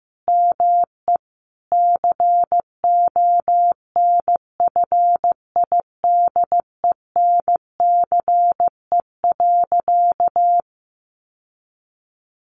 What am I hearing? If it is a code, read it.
Morse code, heard as a keyed tone, about 15 wpm, 700 Hz. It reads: ME CONFIDENCE.
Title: Sentences from Top 1500 Words 15wpm